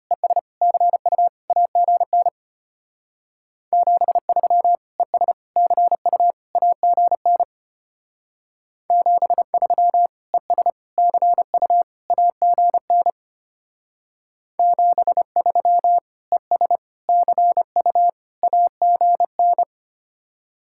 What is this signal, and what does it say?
Morse code, heard as a keyed tone, about 32 wpm, 700 Hz. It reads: ES CU AGN 73 ES CU AGN 73 ES CU AGN 73 ES CU AGN